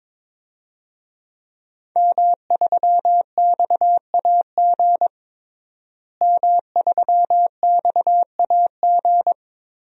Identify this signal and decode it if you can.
Morse code, heard as a keyed tone, about 22 wpm, 700 Hz. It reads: M3XAG M3XAG
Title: DXCC-100 Call Signs - 5 Characters - Repeated 3x - 22wpm